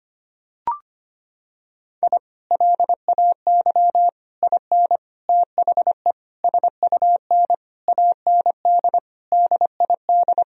Morse code, heard as a keyed tone, about 25 wpm, 700 Hz. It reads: I LAY IN THE SUN AND DID
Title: Sentences from Top 1000 Words - Review Part 2 - 25wpm